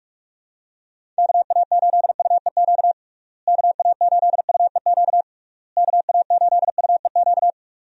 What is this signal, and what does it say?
Morse code, heard as a keyed tone, about 46 wpm, 700 Hz. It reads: KA8UEQ KA8UEQ KA8UEQ
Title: USA Call Signs - 6 Characters - Repeated 3x - 45wpm